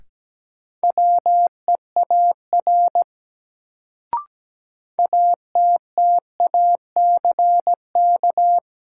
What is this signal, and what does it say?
Morse code, heard as a keyed tone, about 17 wpm, 700 Hz. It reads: WEAR ATTACK
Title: Sentences from Top 1500 Words - New Word Drill Alt - 17wpm